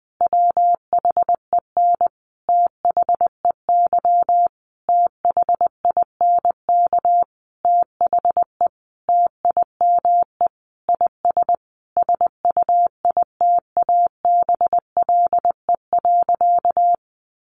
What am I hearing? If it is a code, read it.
Morse code, heard as a keyed tone, about 20 wpm, 700 Hz. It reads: WHEN THEY THINK THE TIME IS SUITABLE.
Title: Sentences from Top 1400 Words 20wpm